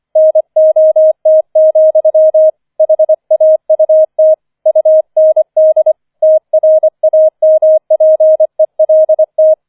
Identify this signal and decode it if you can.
Morse code, heard as a keyed tone, about 25 wpm, 630 Hz. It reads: NOT, HAUT UND TRAMPELT
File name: 005 Fünfter Streich 25wpm